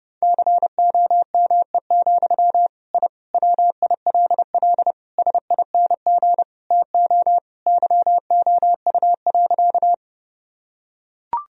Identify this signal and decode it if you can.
Morse code, heard as a keyed tone, about 30 wpm, 700 Hz. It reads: COME, I WILL SING TO YOU.
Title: Sentences from Top 1000 Words - Review Part 2 - 30wpm